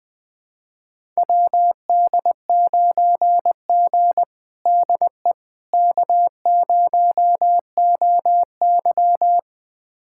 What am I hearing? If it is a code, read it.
Morse code, heard as a keyed tone, about 21 wpm, 700 Hz. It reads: WD9G DE K0OY
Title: USA Callsign DE Callsign - 4 Characters - 20wpm